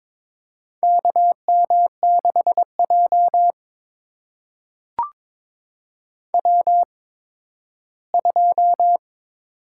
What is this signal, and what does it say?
Morse code, heard as a keyed tone, about 22 wpm, spaced as for 13 wpm, 700 Hz. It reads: KM6J W2
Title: USA Call Signs - 8x Character Spacing - 22wpm